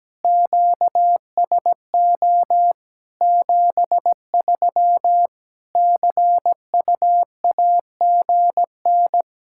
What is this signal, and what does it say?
Morse code, heard as a keyed tone, about 17 wpm, 700 Hz. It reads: QSO 73 CUAGN
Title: K1USN SST QSO Practice 17wpm